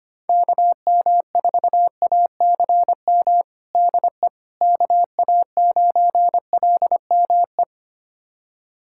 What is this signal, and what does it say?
Morse code, heard as a keyed tone, about 25 wpm, 700 Hz. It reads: KM4ACM DE KA9LME
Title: USA Callsign DE Callsign 25wpm